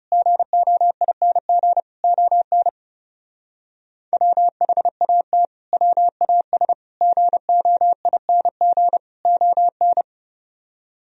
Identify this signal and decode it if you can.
Morse code, heard as a keyed tone, about 35 wpm, 700 Hz. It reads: GOING ON WHAT WAS GOING ON
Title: Speed-Racing - Sets of 4 Words 30wpm